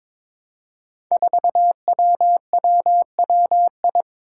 Morse code, heard as a keyed tone, about 22 wpm, 700 Hz. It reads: 4WWWI
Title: Five Challenging Letter-Number Combinations 22wpm